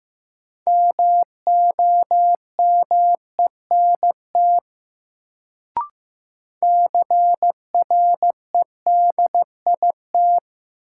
Morse code, heard as a keyed tone, about 15 wpm, 700 Hz. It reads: MOMENT CREDIT
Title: Sentences from Top 1000 Words - New Word Drill - 15wpm